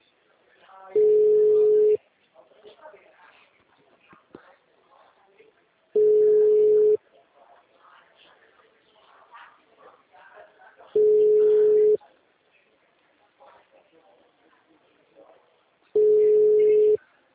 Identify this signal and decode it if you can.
Ringback tone (Europe, 425 Hz), with speech.
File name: out- -2003- - - .14